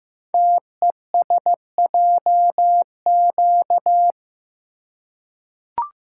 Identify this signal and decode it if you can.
Morse code, heard as a keyed tone, about 15 wpm, 700 Hz. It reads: TESJQ
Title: Code Groups - 5 Letters - 15wpm